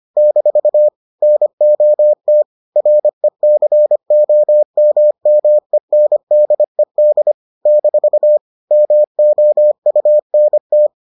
Morse code, heard as a keyed tone, about 25 wpm, 600 Hz. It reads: - NOT RECOMMENDED - MOUNT